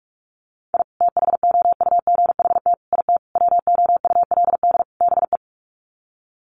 Morse code, heard as a keyed tone, about 44 wpm, 700 Hz. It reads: I THOUGHT IT WOULD BE
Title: Sets of 5 Words Encore 45wpm